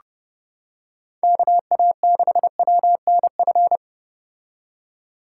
Morse code, heard as a keyed tone, about 30 wpm, 700 Hz. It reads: KA6WNF